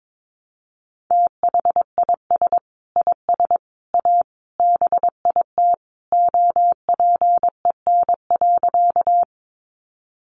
Morse code, heard as a keyed tone, about 22 wpm, 700 Hz. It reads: THIS IS A BIT OPEN.
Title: Sentences from Top 900 Words 22wpm